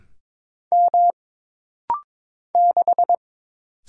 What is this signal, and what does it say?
Morse code, heard as a keyed tone, about 22 wpm, 700 Hz. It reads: M 6
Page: This rapid-fire timing is geared to push your ICR (Instant Character Recognition) skills.